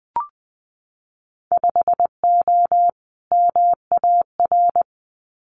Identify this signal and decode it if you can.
Morse code, heard as a keyed tone, about 20 wpm, 700 Hz. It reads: 5O MAR
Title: Winter Field Day Exchanges Encore 20wpm